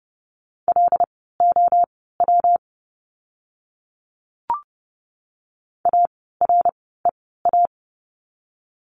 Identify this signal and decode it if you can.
Morse code, heard as a keyed tone, about 30 wpm, spaced as for 17 wpm, 700 Hz. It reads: LOW AREA
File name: course / 30:10wpm / 115 Lesson 19f - Letter w - Words 30wpm